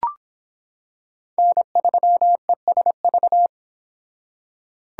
Morse code, heard as a keyed tone, about 26 wpm, 700 Hz. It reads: N3ESV